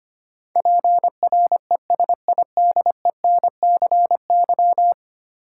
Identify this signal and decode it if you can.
Morse code, heard as a keyed tone, about 25 wpm, 700 Hz. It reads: PRESIDENCY
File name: 10 Letter Words - First 3 Letters Given - 25wpm